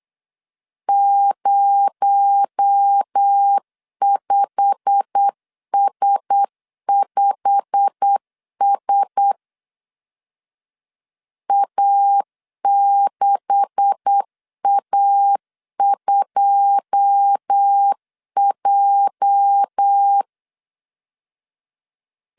Morse code, heard as a keyed tone, about 8 wpm, 800 Hz. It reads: 05S5S A6A2J
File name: CW / DET / 1 9 1500C S B 8wpm 5624